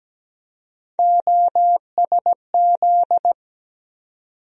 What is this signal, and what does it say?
Morse code, heard as a keyed tone, about 17 wpm, 700 Hz. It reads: OSZ
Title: USA Call Sign Postfixes - 3 Letters 17wpm